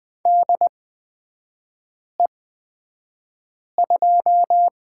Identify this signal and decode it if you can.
Morse code, heard as a keyed tone, about 20 wpm, spaced as for 5 wpm, 690 Hz. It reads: DE2
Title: 255 Lesson 41e - Period - Three Characters 20wpm